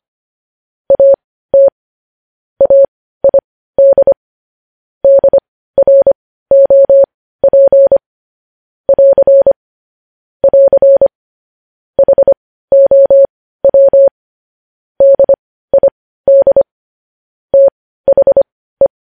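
Morse code, heard as a keyed tone, about 25 wpm, spaced as for 15 wpm, 550 Hz. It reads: AT AID DROP <AR> <AR> HOW DID THE
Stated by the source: News headlines sent in Morse code at 15 words per minute, preceded by vvv.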